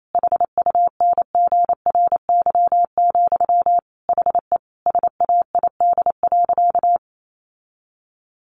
Morse code, heard as a keyed tone, about 28 wpm, 700 Hz. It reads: HUNGRY, HE SAID.
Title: Sentences from Top 1400 Words 28wpm